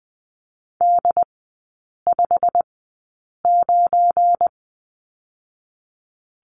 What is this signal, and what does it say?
Morse code, heard as a keyed tone, about 20 wpm, spaced as for 8 wpm, 700 Hz. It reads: D59
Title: 184 Lesson 30e - Letter b - Three Characters 20wpm